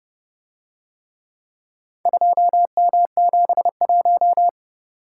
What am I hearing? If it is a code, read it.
Morse code, heard as a keyed tone, about 30 wpm, 700 Hz. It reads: 2M71